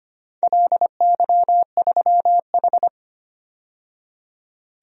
Morse code, heard as a keyed tone, about 25 wpm, 700 Hz. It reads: LY3H
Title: DXCC-100 Call Signs - 4 Characters - 25wpm